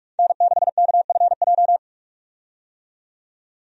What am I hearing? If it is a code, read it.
Morse code, heard as a keyed tone, about 46 wpm, 700 Hz. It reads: N6KFJ
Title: USA Call Signs - 5 Characters - 45wpm